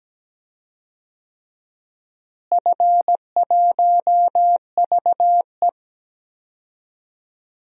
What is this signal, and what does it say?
Morse code, heard as a keyed tone, about 17 wpm, 700 Hz. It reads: F1VE
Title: DXCC-100 Call Signs - 4 Characters - Repeated 3x - 17wpm